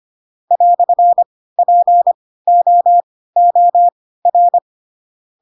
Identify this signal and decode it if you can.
Morse code, heard as a keyed tone, about 25 wpm, spaced as for 16 wpm, 700 Hz. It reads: "POOR
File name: Pinocchio / Pinocchio ch3 10 25 ik1hge